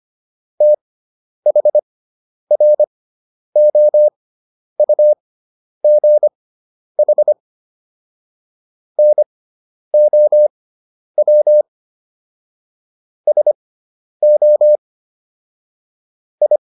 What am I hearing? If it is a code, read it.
Morse code, heard as a keyed tone, about 25 wpm, spaced as for 10 wpm, 600 Hz. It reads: THROUGH NOW SO I
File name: QSO 410 10